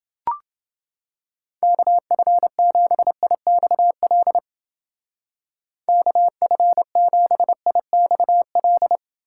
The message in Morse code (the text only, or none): KF7IXL KF7IXL